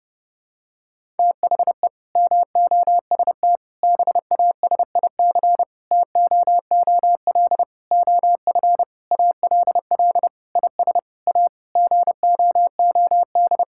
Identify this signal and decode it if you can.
Morse code, heard as a keyed tone, about 32 wpm, 700 Hz. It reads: THE MOST BASIC TOOL OF ALL IS A GOOD
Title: Sentences from Top 1000 Words - Review Part 6 - 30wpm